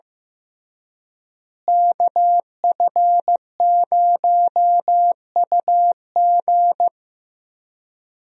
Morse code, heard as a keyed tone, about 16 wpm, 700 Hz. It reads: KF0UG
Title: USA Call Signs - 5 Characters - Repeated 3x - 15wpm